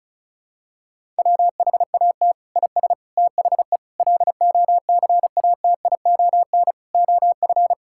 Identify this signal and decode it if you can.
Morse code, heard as a keyed tone, about 36 wpm, 700 Hz. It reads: WHAT IS THE LOCATION OF
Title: Sentences from Top 1000 Words - Review Part 6 - 35wpm